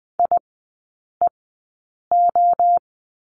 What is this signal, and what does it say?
Morse code, heard as a keyed tone, about 20 wpm, spaced as for 8 wpm, 700 Hz. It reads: IEO
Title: 035 Lesson 7e - Letter s - Three Characters 20wpm